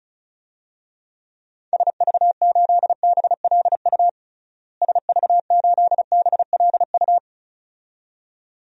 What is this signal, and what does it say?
Morse code, heard as a keyed tone, about 36 wpm, 700 Hz. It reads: SV8BLU SV8BLU
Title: DXCC-100 Call Signs - 6 Characters - Repeated 2x - 35wpm